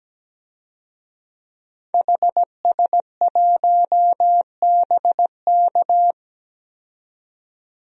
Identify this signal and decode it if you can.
Morse code, heard as a keyed tone, about 17 wpm, 700 Hz. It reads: HS1BK
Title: DXCC-100 Call Signs - 5 Characters - 17wpm